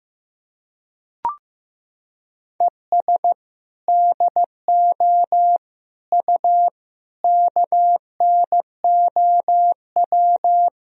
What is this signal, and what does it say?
Morse code, heard as a keyed tone, about 16 wpm, 700 Hz. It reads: ES DO U KNOW
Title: Sentences from Top 200 Words and Abbreviations 15wpm